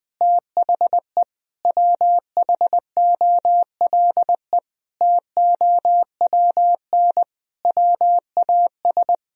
Morse code, heard as a keyed tone, about 20 wpm, 700 Hz. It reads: THE WHOLE TOWN WAS
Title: Sentences from Top 1000 Words - Review Part 2 - 20wpm